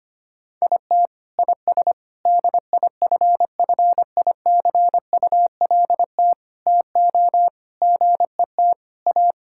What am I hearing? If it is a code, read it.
Morse code, heard as a keyed tone, about 25 wpm, 700 Hz. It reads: IT IS DIFFICULT TO GET A
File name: Sets of 9 Words - 25wpm